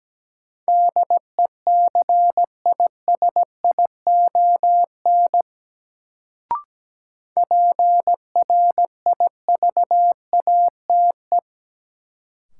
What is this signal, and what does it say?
Morse code, heard as a keyed tone, about 17 wpm, 700 Hz. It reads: DECISION PRIVATE
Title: Sentences from Top 1000 Words - New Word Drill - 17wpm